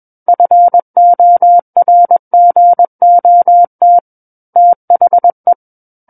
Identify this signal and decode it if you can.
Morse code, heard as a keyed tone, about 21 wpm, 700 Hz. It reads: FORGOT THE
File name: Book One / 12-Chapter 12 - 21 WPM